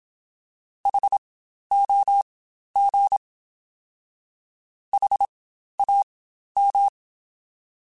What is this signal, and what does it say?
Morse code, heard as a keyed tone, about 27 wpm, spaced as for 12 wpm, 800 Hz. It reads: HOG HAM
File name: Spaced Alphas / Spaced Alphas mp3s / Hs sp @25